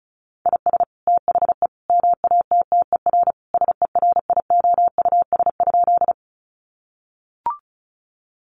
Morse code, heard as a keyed tone, about 34 wpm, 700 Hz. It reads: IS THE MATTER SERIOUS?
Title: Sentences from Top 1100 Words 35wpm